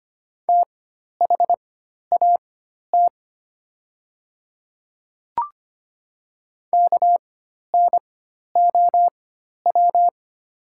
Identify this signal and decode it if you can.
Morse code, heard as a keyed tone, about 25 wpm, spaced as for 12 wpm, 700 Hz. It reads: THAT KNOW
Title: 4x Character Spacing - Top 500 Words - 25wpm